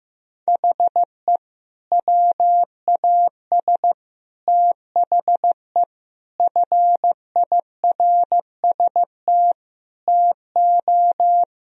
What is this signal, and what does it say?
Morse code, heard as a keyed tone, about 15 wpm, 700 Hz. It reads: HE WAS THE FIRST TO
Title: Sentences from Top 1000 Words - Review Part 1 - 15wpm